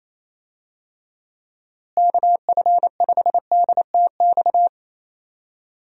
Morse code, heard as a keyed tone, about 27 wpm, 700 Hz. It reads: KF5DTX
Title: USA Call Signs - 6 Characters - Repeated 3x - 28wpm